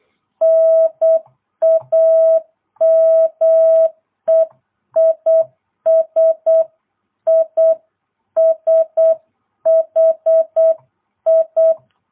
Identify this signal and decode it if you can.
Morse code, heard as a keyed tone, about 8 wpm, 650 Hz. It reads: NAME IS ISHI